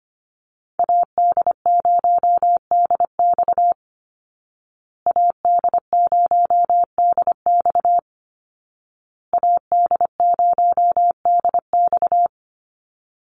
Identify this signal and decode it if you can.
Morse code, heard as a keyed tone, about 25 wpm, 700 Hz. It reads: AD0DX AD0DX AD0DX
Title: Call Signs Repeated 2x - 25wpm